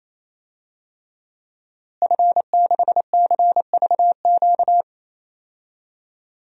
Morse code, heard as a keyed tone, about 28 wpm, 700 Hz. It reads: F6CVQ
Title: DXCC-100 Call Signs - 5 Characters - Repeated 3x - 28wpm